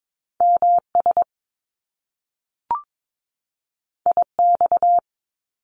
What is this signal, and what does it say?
Morse code, heard as a keyed tone, about 22 wpm, 700 Hz. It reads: MS IX
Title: Top Two Letter Combinations in Words 22wpm